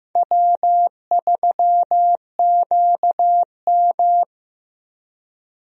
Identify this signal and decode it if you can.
Morse code, heard as a keyed tone, about 16 wpm, 700 Hz. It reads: W3QM
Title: USA Call Signs - 4 Characters - Repeated 3x - 15wpm